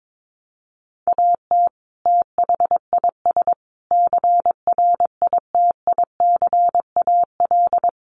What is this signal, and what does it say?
Morse code, heard as a keyed tone, about 22 wpm, 700 Hz. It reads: AT THIS CRITICAL
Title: Sentences from Top 1100 Words 22wpm